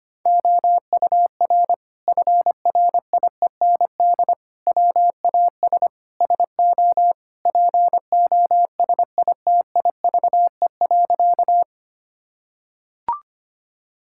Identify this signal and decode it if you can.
Morse code, heard as a keyed tone, about 25 wpm, 700 Hz. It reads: OUR FRIEND WAS SO POSITIVE.
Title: Sentences from Top 1000 Words - Review Part 5 - 25wpm